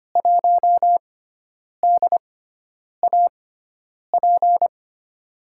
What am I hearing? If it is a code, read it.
Morse code, heard as a keyed tone, about 25 wpm, spaced as for 9 wpm, 700 Hz. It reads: 1DAP
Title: USA Call Signs - 5 Characters - 6x Character Spacing - 25wpm